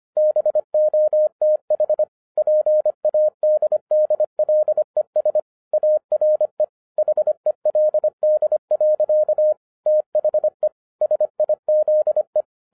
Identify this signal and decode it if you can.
Morse code, heard as a keyed tone, about 25 wpm, 610 Hz. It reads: BOTH PADDLES ARE HELD. THE SIZE